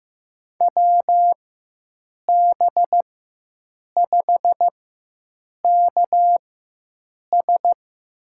Morse code, heard as a keyed tone, about 15 wpm, spaced as for 7 wpm, 700 Hz. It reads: WB5KS
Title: USA Call Signs - 6 Characters - 4x Character Spacing - 15wpm